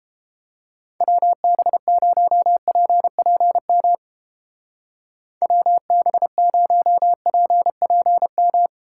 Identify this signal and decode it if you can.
Morse code, heard as a keyed tone, about 29 wpm, 700 Hz. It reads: WB0PPM WB0PPM